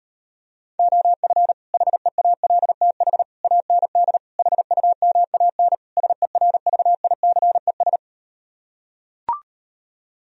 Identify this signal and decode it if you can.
Morse code, heard as a keyed tone, about 38 wpm, 700 Hz. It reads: OF HEALTH AND HUMAN SERVICES